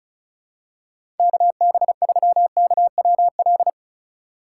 Morse code, heard as a keyed tone, about 36 wpm, 700 Hz. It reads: KB3KWL
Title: USA Call Signs - 4x Character Spacing - 35wpm